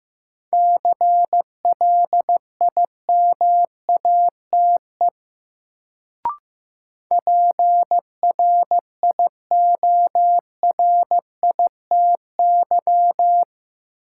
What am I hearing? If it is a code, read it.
Morse code, heard as a keyed tone, about 15 wpm, 700 Hz. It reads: CLIMATE PRIORITY
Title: Sentences from Top 1600 Words - New Word Drill - 15wpm